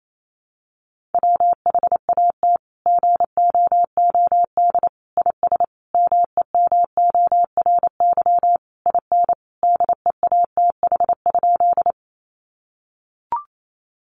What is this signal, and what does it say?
Morse code, heard as a keyed tone, about 28 wpm, 700 Hz. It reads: WHAT GOOD IS MEMORY IN DEATH?
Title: Sentences from Top 1300 Words 28wpm